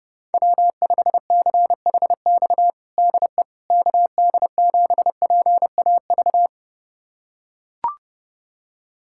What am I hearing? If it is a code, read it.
Morse code, heard as a keyed tone, about 30 wpm, 700 Hz. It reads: W5CHX DE KD7PAV
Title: USA Callsign DE Callsign 30wpm